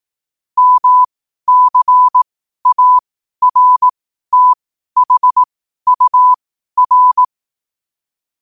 Morse code, heard as a keyed tone, about 18 wpm, spaced as for 12 wpm, 1000 Hz. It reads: MCARTHUR
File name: qotdfiles / 12wpm